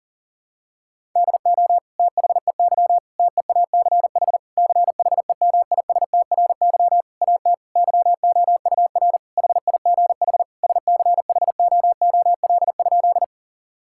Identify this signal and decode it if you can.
Morse code, heard as a keyed tone, about 40 wpm, 700 Hz. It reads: DO THEY TEACH CHEMISTRY AT YOUR HIGH SCHOOL?
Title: Sentences from Top 1000 Words - Review Part 6 - 40wpm